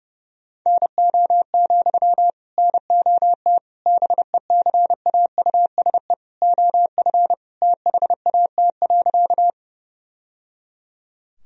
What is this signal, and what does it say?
Morse code, heard as a keyed tone, about 30 wpm, 700 Hz. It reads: NO, NOT BECAUSE OF THAT.
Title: Sentences from Top 1000 Words - Review Part 3 - 30wpm